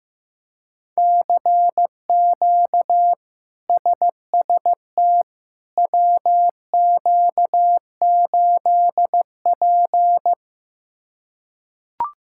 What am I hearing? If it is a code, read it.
Morse code, heard as a keyed tone, about 15 wpm, 700 Hz. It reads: CQ SST WQ8P
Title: K1USN SST QSO Practice - 15wpm